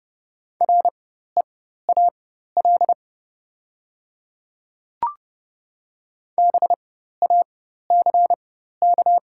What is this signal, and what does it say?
Morse code, heard as a keyed tone, about 31 wpm, spaced as for 14 wpm, 700 Hz. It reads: REAL BACK
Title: 4x Character Spacing - Top 500 Words - 30wpm